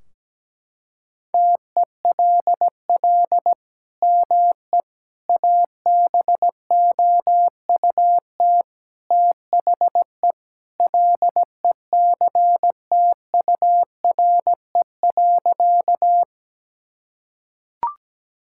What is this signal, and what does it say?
Morse code, heard as a keyed tone, about 17 wpm, 700 Hz. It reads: TELL ME ABOUT THE LECTURE.
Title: Sentences from Top 1600 Words 17wpm